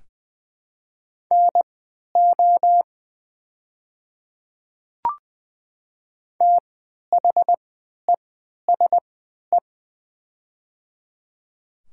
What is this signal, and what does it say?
Morse code, heard as a keyed tone, about 20 wpm, spaced as for 11 wpm, 700 Hz. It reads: NO THESE
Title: 3x Character Spacing - Top 500 Words - 20wpm